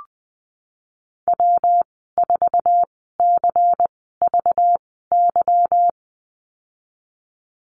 Morse code, heard as a keyed tone, about 19 wpm, spaced as for 14 wpm, 700 Hz. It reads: W4CVY
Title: 193 Lesson 31g - Letter v - Call Signs 20wpm